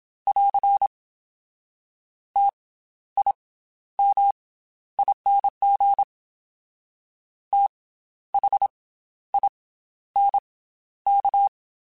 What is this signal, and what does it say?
Morse code, heard as a keyed tone, about 27 wpm, 800 Hz. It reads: <AR> T I M ING T H I N K
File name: Special INGs / Special INGs-mp3s / Slow ing @25